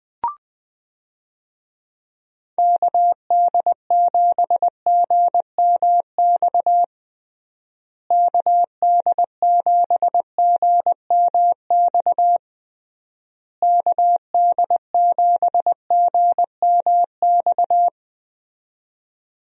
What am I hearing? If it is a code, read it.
Morse code, heard as a keyed tone, about 21 wpm, 700 Hz. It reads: KD7GMX KD7GMX KD7GMX
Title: USA Call Signs - 6 Characters - Repeated 3x - 20wpm